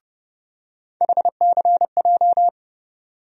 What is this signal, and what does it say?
Morse code, heard as a keyed tone, about 30 wpm, 700 Hz. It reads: HCJ
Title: USA Call Sign Postfixes - 3 Letters 30wpm